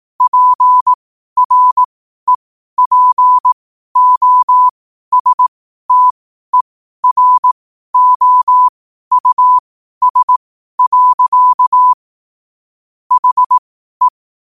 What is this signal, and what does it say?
Morse code, heard as a keyed tone, about 18 wpm, spaced as for 12 wpm, 1000 Hz. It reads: PREPOSTEROUS. HE
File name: 12wpm